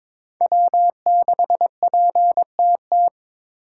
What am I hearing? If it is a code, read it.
Morse code, heard as a keyed tone, about 22 wpm, 700 Hz. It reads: W6PTT
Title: USA Call Signs - 5 Characters - 4x Character Spacing - 22wpm